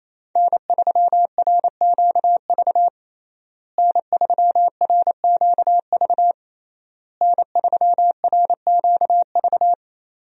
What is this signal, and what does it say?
Morse code, heard as a keyed tone, about 27 wpm, 700 Hz. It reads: N3RQV N3RQV N3RQV
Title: USA Call Signs - 5 Characters - Repeated 3x - 28wpm